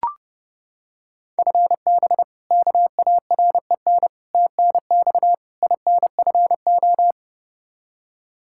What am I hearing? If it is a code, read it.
Morse code, heard as a keyed tone, about 30 wpm, 700 Hz. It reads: FB KAREN TNX INFO
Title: K1USN SST QSO Practice 30wpm